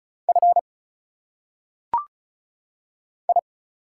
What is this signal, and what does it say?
Morse code, heard as a keyed tone, about 34 wpm, 700 Hz. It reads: F I
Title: Single Letters - Phonetic Pronunciation 35wpm